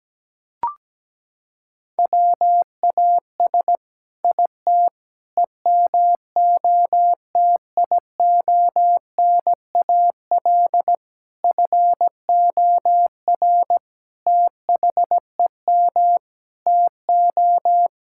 Morse code, heard as a keyed tone, about 17 wpm, 700 Hz. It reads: WAS IT EMOTIONAL FOR THEM TO
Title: Sentences from Top 1000 Words - Review Part 5 - 17wpm